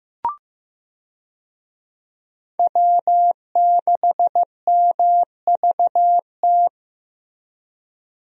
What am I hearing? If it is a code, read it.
Morse code, heard as a keyed tone, about 15 wpm, 700 Hz. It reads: W6MVT
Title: USA Call Signs - 5 Characters - Repeated 2x - 15wpm